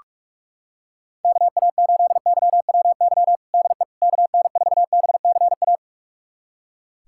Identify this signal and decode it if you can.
Morse code, heard as a keyed tone, about 42 wpm, 700 Hz. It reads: KA8YWY DE KN4BCA
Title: USA Callsign DE Callsign - 6 Characters - 45wpm